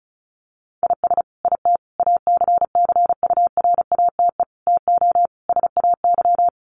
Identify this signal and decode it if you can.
Morse code, heard as a keyed tone, about 34 wpm, 700 Hz. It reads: IS IT ACCURATE TO SAY
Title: Sentences from Top 1200 Words 35wpm